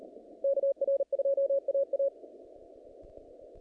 Morse code, heard as a keyed tone, about 40 wpm, 550 Hz. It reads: KR2AA
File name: contests / naqp-cw-k5tr-op-k5pi- -2200